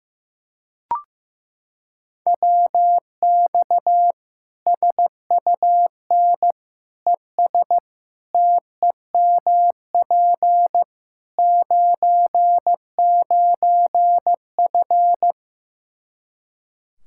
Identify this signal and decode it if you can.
Morse code, heard as a keyed tone, about 15 wpm, 700 Hz. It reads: WX SUN ES TEMP 99F
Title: Practice QSO - Exchange 2 - Info - 15wpm